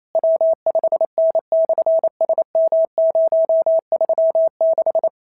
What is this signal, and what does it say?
Morse code, heard as a keyed tone, about 28 wpm, 650 Hz. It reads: W5N/SM036
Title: SOTA QSO Practice 28wpm